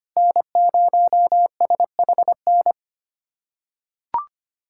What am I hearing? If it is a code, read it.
Morse code, heard as a keyed tone, about 25 wpm, 700 Hz. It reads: N0SHN
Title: USA Callsign DE Callsign - 5 Characters - 25wpm